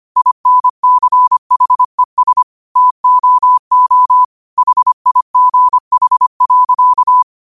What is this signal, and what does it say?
Morse code, heard as a keyed tone, about 25 wpm, 1000 Hz. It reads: INCHES TOO HIGH.
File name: qotdfiles / 25wpm